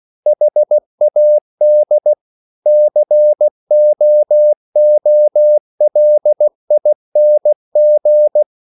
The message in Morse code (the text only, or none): HAD COOLING